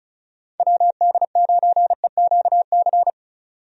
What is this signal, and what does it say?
Morse code, heard as a keyed tone, about 34 wpm, 700 Hz. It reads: WD9EQC
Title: USA Call Signs - 6 Characters - 35wpm